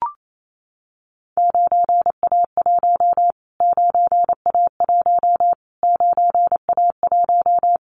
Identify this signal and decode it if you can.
Morse code, heard as a keyed tone, about 28 wpm, 700 Hz. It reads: 9A1 9A1 9A1
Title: Callsign Prefixes for Top 25 DXCC - Repeated 3x - 28wpm